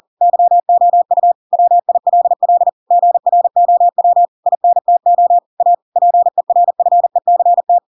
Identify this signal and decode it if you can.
Morse code, heard as a keyed tone, about 40 wpm, 700 Hz. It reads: YOU WILL GROW INTO A PERFECT
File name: Pinocchio / Pinocchio ch4 40 40 ik1hge